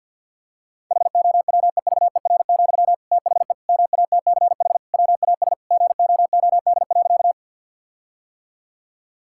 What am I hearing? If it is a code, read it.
Morse code, heard as a keyed tone, about 50 wpm, 700 Hz. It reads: HOWEVER, THE MATCH WAS GOOD.
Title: Sentences from Top 1600 Words 50wpm